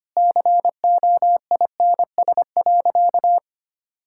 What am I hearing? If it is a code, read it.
Morse code, heard as a keyed tone, about 25 wpm, 700 Hz. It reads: COINS.
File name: Easy Sentences with Hints - 25wpm